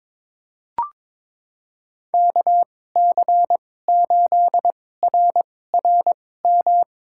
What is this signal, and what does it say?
Morse code, heard as a keyed tone, about 22 wpm, spaced as for 16 wpm, 700 Hz. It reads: KC8RRM
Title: USA Call Signs - 6 Characters - 2x Character Spacing - 22wpm